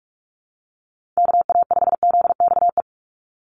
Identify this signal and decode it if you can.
Morse code, heard as a keyed tone, about 44 wpm, 700 Hz. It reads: KA5ZXE
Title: USA Call Signs 45wpm